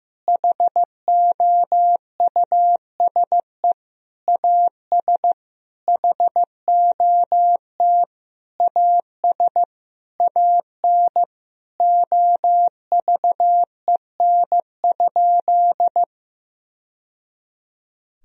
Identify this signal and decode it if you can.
Morse code, heard as a keyed tone, about 15 wpm, 700 Hz. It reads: HOUSE AS HOT AS AN OVEN?
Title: Sentences from Top 1000 Words - Review Part 4 - 15wpm